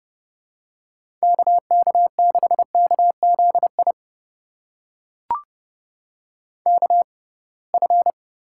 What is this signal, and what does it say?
Morse code, heard as a keyed tone, about 30 wpm, spaced as for 23 wpm, 700 Hz. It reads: KK6KZI KF
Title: USA Call Signs - 6x Character Spacing - 30wpm